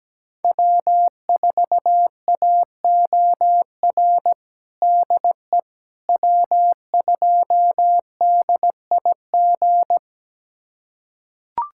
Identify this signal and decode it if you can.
Morse code, heard as a keyed tone, about 17 wpm, 700 Hz. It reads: W4AOR DE W2DIG
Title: USA Callsign DE Callsign - 5 Characters - 17wpm